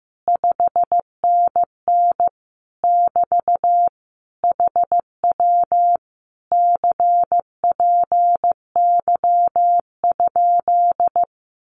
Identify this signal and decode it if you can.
Morse code, heard as a keyed tone, about 15 wpm, 700 Hz. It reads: 5NN <BT> HW CPY?
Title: Common QSO Phrases Encore 15wpm